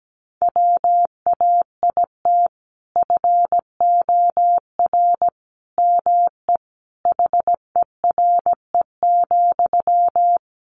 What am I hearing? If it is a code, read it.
Morse code, heard as a keyed tone, about 17 wpm, 700 Hz. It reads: WAIT FOR ME HERE,
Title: Sentences from Top 500 Words 17wpm